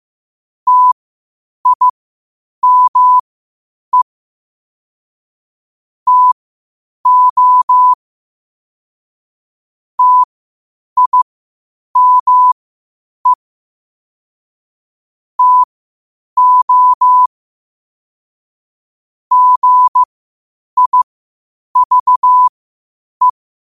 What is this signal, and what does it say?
Morse code, heard as a keyed tone, about 15 wpm, spaced as for 8 wpm, 1000 Hz. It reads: TIME TO TIME TO GIVE
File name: qotdfiles / 8wpm